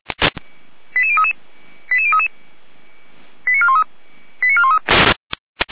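Five-tone selective call (ZVEI), heard as quick runs of five tones, tones of about 70 ms. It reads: 88033 88033 89413 89413
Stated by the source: ZVEI five-tone call received by SDR.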